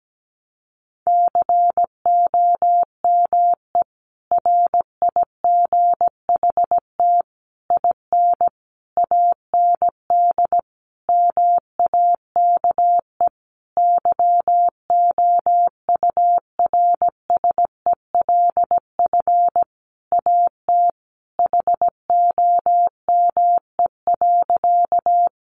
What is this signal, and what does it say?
Morse code, heard as a keyed tone, about 17 wpm, 700 Hz. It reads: COME RIGHT IN AND MAKE YOURSELF AT HOME.
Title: Sentences from Top 600 Words 17wpm